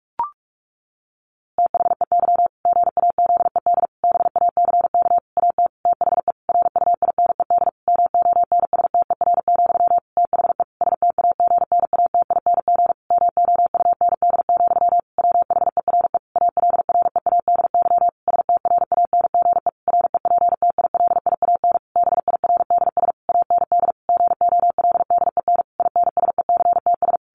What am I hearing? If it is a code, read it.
Morse code, heard as a keyed tone, about 44 wpm, 700 Hz. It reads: THEY GAZED BACK AT THE RUINED MONSTER, THE STAGNATING MOUND, WHERE ALREADY STRANGE REPTILIAN BIRDS AND GOLDEN INSECTS
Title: A Sound Of Thunder 45wpm